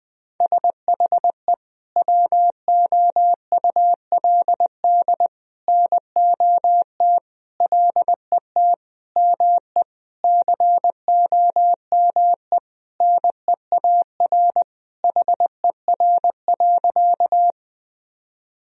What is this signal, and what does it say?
Morse code, heard as a keyed tone, about 20 wpm, 700 Hz. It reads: SHE WOULD NOT LET ME COME NEAR HER.
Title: Sentences from Top 1000 Words - Review Part 1 - 20wpm